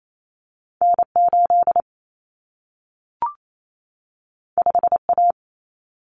Morse code, heard as a keyed tone, about 28 wpm, 700 Hz. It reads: N8 5A
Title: Common Cut Numbers - 2-Digit 28wpm